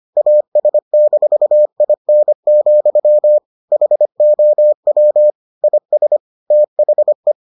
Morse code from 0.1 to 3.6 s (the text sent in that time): AS-IN,